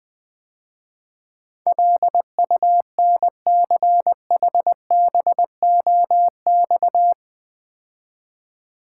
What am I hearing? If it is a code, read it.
Morse code, heard as a keyed tone, about 20 wpm, 700 Hz. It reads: LUNCHBOX
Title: Easy Words - Speed-Racing with Hints - 20wpm